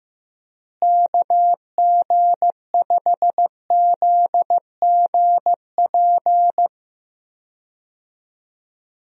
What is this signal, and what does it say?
Morse code, heard as a keyed tone, about 15 wpm, 700 Hz. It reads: KG5ZGP
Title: USA Call Signs - 6 Characters - Repeated 3x - 15wpm